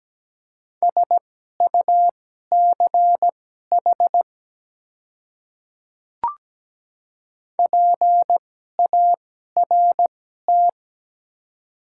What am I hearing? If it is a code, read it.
Morse code, heard as a keyed tone, about 17 wpm, spaced as for 12 wpm, 690 Hz. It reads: SUCH PART
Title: 2x Character Spacing - Top 300 Words - 17wpm